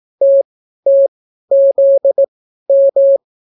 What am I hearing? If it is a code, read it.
Morse code, heard as a keyed tone, about 18 wpm, spaced as for 12 wpm, 550 Hz. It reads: TTZM